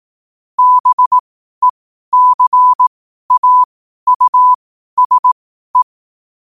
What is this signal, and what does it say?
Morse code, heard as a keyed tone, about 18 wpm, spaced as for 12 wpm, 1000 Hz. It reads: BECAUSE
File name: qotdfiles / 12wpm